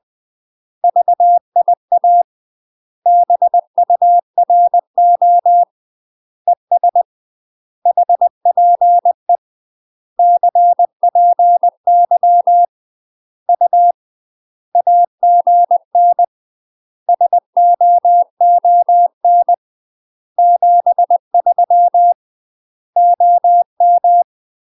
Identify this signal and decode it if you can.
Morse code, heard as a keyed tone, about 20 wpm, 700 Hz. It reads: VIA BURO ES HPE CPY U AGN SOON 73 OM
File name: UK MASTER QSOs TRAINING / 20WPM / UK MASTER QSOs 0070